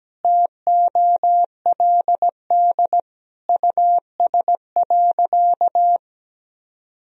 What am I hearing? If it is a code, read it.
Morse code, heard as a keyed tone, about 17 wpm, 700 Hz. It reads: TOLD US.
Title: Sentences from Top 1000 Words - Review Part 3 - 17wpm